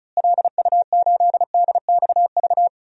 Morse code, heard as a keyed tone, about 34 wpm, 700 Hz. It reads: LU8DXV
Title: DXCC-100 Call Signs - 6 Characters - Repeated 2x - 35wpm